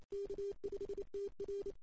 Morse code, heard as a keyed tone, about 28 wpm, 410 Hz. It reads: K5TR
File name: CW / 2008-ss-cw-k5tr-op-n5rz -05:00